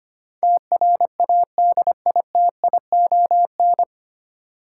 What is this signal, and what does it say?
Morse code, heard as a keyed tone, about 25 wpm, 700 Hz. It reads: TRADITION
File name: Nine Letter Words - 25wpm